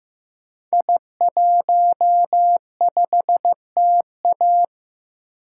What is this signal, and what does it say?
Morse code, heard as a keyed tone, about 15 wpm, 700 Hz. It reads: I15TA
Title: Code Groups - 5 Letter:Numbers - 15wpm